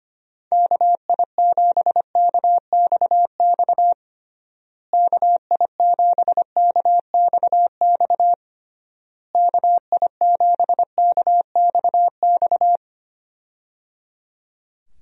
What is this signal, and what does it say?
Morse code, heard as a keyed tone, about 25 wpm, 700 Hz. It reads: KI7KXX KI7KXX KI7KXX
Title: USA Call Signs - 6 Characters - Repeated 3x - 25wpm